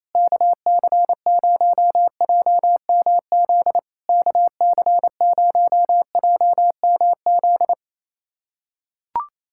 Morse code, heard as a keyed tone, about 28 wpm, 700 Hz. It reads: KC0JMZ KC0JMZ
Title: SOTA QSO Practice 28wpm